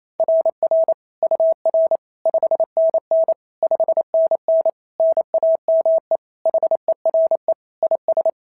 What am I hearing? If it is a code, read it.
Morse code, heard as a keyed tone, about 27 wpm, 650 Hz. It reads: RR UR 5NN 5NN NAME HERE IS
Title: SOTA QSO Practice 28wpm